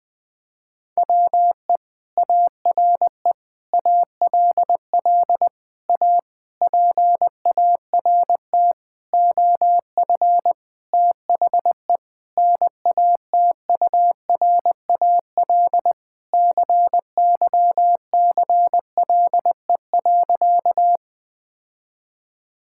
Morse code, heard as a keyed tone, about 20 wpm, 700 Hz. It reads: WE ARE ALL A PART OF THE NATURAL CYCLE.
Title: Sentences from Top 1000 Words - Review Part 6 - 20wpm